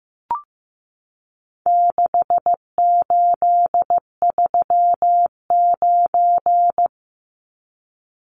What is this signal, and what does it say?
Morse code, heard as a keyed tone, about 15 wpm, 700 Hz. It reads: 6839
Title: Numbers - 4-Digit 15wpm